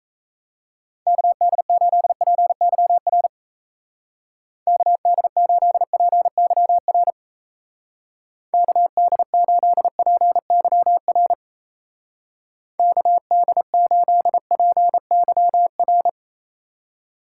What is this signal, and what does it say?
Morse code, heard as a keyed tone, about 34 wpm, 700 Hz. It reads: KD8PYR KD8PYR KD8PYR KD8PYR
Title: Speed Racing - USA 2x3 Call Signs 28wpm